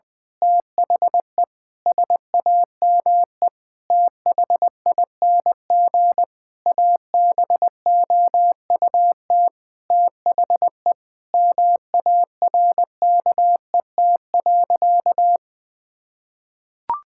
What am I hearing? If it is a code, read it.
Morse code, heard as a keyed tone, about 20 wpm, 700 Hz. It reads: THE SAME THING ABOUT THE MARKET.
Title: Sentences from Top 1000 Words - Review Part 5 - 20wpm